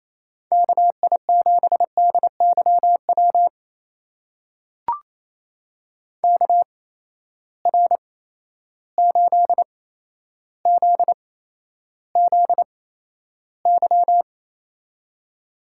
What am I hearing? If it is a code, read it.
Morse code, heard as a keyed tone, about 28 wpm, spaced as for 12 wpm, 700 Hz. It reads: KI7DYW KR8ZZY
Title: USA Call Signs - 8x Character Spacing - 28wpm